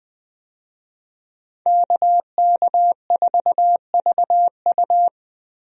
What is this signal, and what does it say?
Morse code, heard as a keyed tone, about 20 wpm, 700 Hz. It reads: KK4VU
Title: USA Call Signs - 5 Characters - Repeated 2x - 20wpm